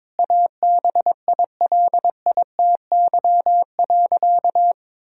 Morse code, heard as a keyed tone, about 22 wpm, 700 Hz. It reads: ABILITY.
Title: Sentences from Top 1000 Words - Review Part 3 - 22wpm